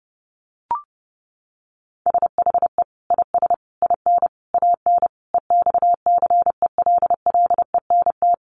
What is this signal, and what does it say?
Morse code, heard as a keyed tone, about 30 wpm, 700 Hz. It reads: SHE IS IN AN EXCELLENT
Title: Sentences from Top 1000 Words 30wpm